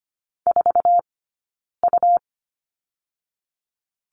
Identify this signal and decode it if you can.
Morse code, heard as a keyed tone, about 25 wpm, spaced as for 9 wpm, 700 Hz. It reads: 4U
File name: Lesson 17d - Letter u - Two Characters 25wpm